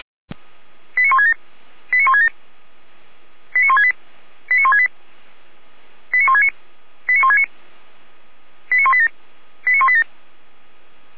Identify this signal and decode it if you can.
Five-tone selective call (ZVEI), heard as quick runs of five tones, tones of about 80 ms. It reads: 89167 89167 89168 89168 89169 89169 89187 89187